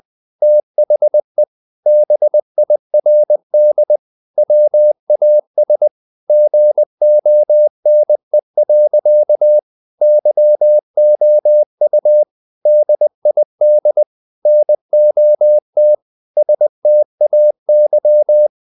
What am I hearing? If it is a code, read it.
Morse code, heard as a keyed tone, about 20 wpm, 600 Hz. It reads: THE BIRD WAS GONE. YOU DID NOT STAY
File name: the old man and the sea / the old man and the sea part 1 20wpm